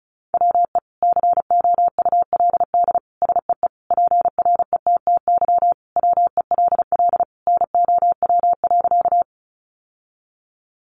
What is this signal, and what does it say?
Morse code, heard as a keyed tone, about 34 wpm, 700 Hz. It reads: WE COULD SEE PRETTY WELL NOW.
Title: Sentences from Top 1000 Words 35wpm